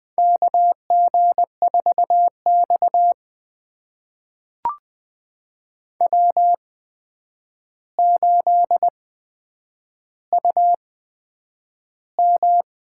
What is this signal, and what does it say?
Morse code, heard as a keyed tone, about 20 wpm, spaced as for 9 wpm, 700 Hz. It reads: KG4X W8UM
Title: USA Call Signs - 4 Characters - 8x Character Spacing - 20wpm